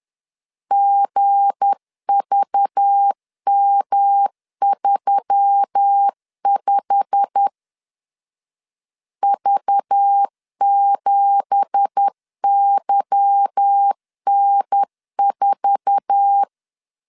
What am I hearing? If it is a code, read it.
Morse code, heard as a keyed tone, about 10 wpm, 800 Hz. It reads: GVM35 V7YN4
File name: CW / DET / 1 9 1500C S B 10wpm 4439